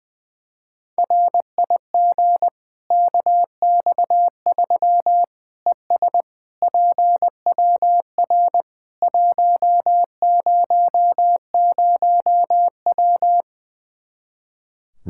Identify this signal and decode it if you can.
Morse code, heard as a keyed tone, about 21 wpm, 700 Hz. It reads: RIG KX3 ES PWR 100W
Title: Practice QSO - Exchange 2 - Info - 20wpm